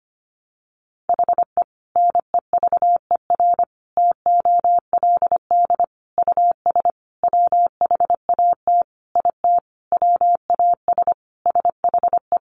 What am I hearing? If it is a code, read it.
Morse code, heard as a keyed tone, about 25 wpm, 700 Hz. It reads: HE NEVER TOLD US WHAT IT WAS SHE
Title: Sentences from Top 400 Words 25wpm